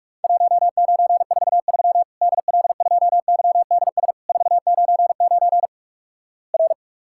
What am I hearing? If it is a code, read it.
Morse code, heard as a keyed tone, about 46 wpm, 700 Hz. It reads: 1943 DL2YDS 499 R
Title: Worked All Europe DX Contest - QTC Exchange - 45wpm